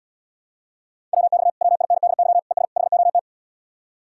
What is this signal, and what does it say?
Morse code, heard as a keyed tone, about 80 wpm, 700 Hz. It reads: WX WINDY ES HOT